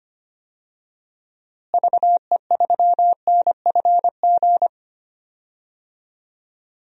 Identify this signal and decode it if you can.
Morse code, heard as a keyed tone, about 25 wpm, 700 Hz. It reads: VE3NFG
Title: DXCC-100 Call Signs - 6 Characters - Repeated 2x - 25wpm